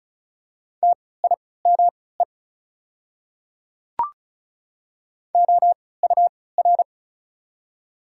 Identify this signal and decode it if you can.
Morse code, heard as a keyed tone, about 36 wpm, spaced as for 20 wpm, 690 Hz. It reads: TIME OUR
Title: 3x Character Spacing - Top 200 Words - 35wpm